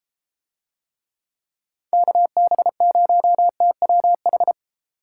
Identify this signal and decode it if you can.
Morse code, heard as a keyed tone, about 32 wpm, 700 Hz. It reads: KB0TWH